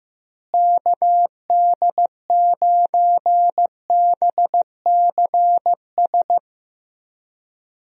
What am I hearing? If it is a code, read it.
Morse code, heard as a keyed tone, about 15 wpm, 700 Hz. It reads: KD9BCS
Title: USA Call Signs - 6 Characters - Repeated 3x - 15wpm